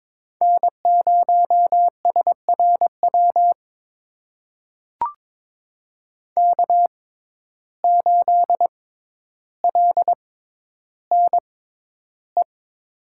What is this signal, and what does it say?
Morse code, heard as a keyed tone, about 22 wpm, spaced as for 11 wpm, 700 Hz. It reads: N0SRW K8LNE
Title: USA Call Signs - 5 Characters - 6x Character Spacing - 22wpm